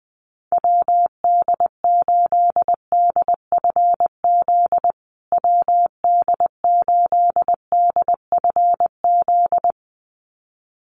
Morse code, heard as a keyed tone, about 20 wpm, 700 Hz. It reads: WD8DFZ WD8DFZ
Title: USA Call Signs Repeated 2x 20wpm